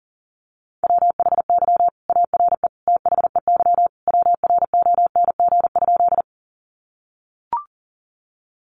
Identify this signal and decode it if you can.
Morse code, heard as a keyed tone, about 40 wpm, 700 Hz. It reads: WHY ARE THEY WRONG?
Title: Sentences from Top 900 Words 40wpm